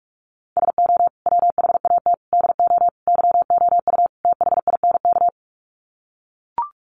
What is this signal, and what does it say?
Morse code, heard as a keyed tone, about 44 wpm, 700 Hz. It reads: SO WHAT DO YOU THINK
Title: Sets of 5 Words Encore2 45wpm